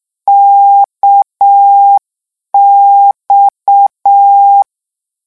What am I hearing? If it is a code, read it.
Morse code, heard as a keyed tone, about 6 wpm, 800 Hz. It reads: KX
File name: morse mp3 / lecon 1-vitesse 6